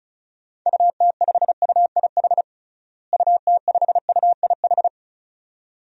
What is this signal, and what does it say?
Morse code, heard as a keyed tone, about 34 wpm, 700 Hz. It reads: UT5UIH UT5UIH
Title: DXCC-100 Call Signs - 6 Characters - Repeated 2x - 35wpm